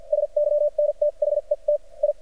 Morse code, heard as a keyed tone, about 48 wpm, spaced as for 36 wpm, 590 Hz. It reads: SYNTHETI
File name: archived / CRSnet